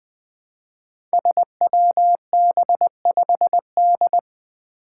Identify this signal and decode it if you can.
Morse code, heard as a keyed tone, about 20 wpm, 700 Hz. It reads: SWB5D
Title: Five Challenging Letter-Number Combinations 20wpm